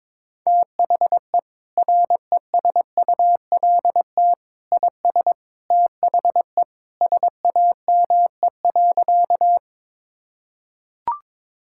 Morse code, heard as a keyed tone, about 21 wpm, 700 Hz. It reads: THE RESULT IS THE SAME.
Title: Sentences from Top 1000 Words - Review Part 5 - 22wpm